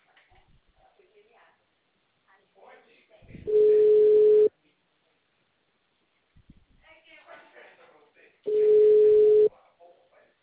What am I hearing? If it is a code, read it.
Ringback tone (Europe, 425 Hz), with speech.
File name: out- -2012- - - .91687